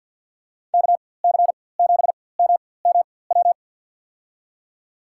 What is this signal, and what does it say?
Morse code, heard as a keyed tone, about 48 wpm, spaced as for 23 wpm, 700 Hz. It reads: KC7MMW
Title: USA Call Signs - 6 Characters - 4x Character Spacing - 50wpm